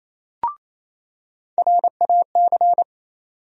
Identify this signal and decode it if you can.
Morse code, heard as a keyed tone, about 28 wpm, 700 Hz. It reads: RAC
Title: USA Call Sign Postfixes - 3 Letters 28wpm